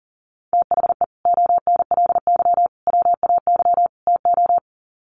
Morse code, heard as a keyed tone, about 40 wpm, 700 Hz. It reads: THE ONLY WAY TO
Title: Sets of 4 Words 40wpm